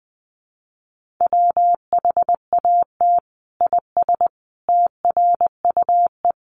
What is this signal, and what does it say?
Morse code, heard as a keyed tone, about 20 wpm, 700 Hz. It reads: WHAT IS TRUE
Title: Sentences from Top 1400 Words 20wpm